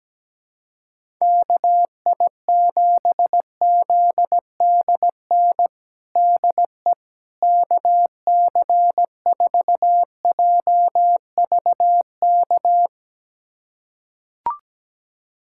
Morse code, heard as a keyed tone, about 17 wpm, 700 Hz. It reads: KI7ZDN DE KC4JVK
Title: USA Callsign DE Callsign - 6 Characters - 17wpm